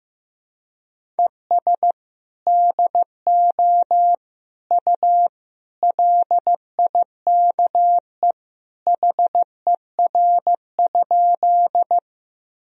Morse code, heard as a keyed tone, about 15 wpm, 700 Hz. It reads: ES DO U LIKE HER?
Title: Sentences from Top 100 Words and Abbreviations 15wpm